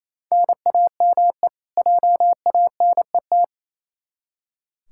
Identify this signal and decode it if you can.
Morse code, heard as a keyed tone, about 28 wpm, 700 Hz. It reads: NAME JANET
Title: K1USN SST QSO Practice 28wpm